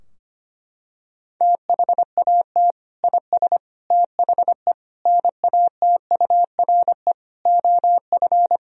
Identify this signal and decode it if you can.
Morse code, heard as a keyed tone, about 25 wpm, 700 Hz. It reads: THAT IS THE NATURE OF
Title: Sentences from Top 1000 Words - Review Part 3 - 25wpm